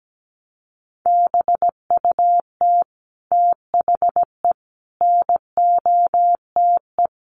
Morse code, heard as a keyed tone, about 17 wpm, 700 Hz. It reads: BUT THE NOTE
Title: Sentences from Top 800 Words 17wpm